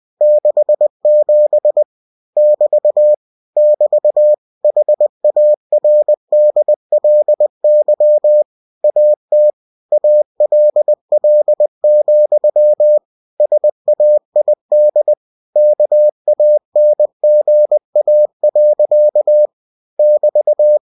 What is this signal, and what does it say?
Morse code, heard as a keyed tone, about 20 wpm, 600 Hz. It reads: 67 <BT> <BT> HARDLY AT ALL, SAID KANGA. <BT>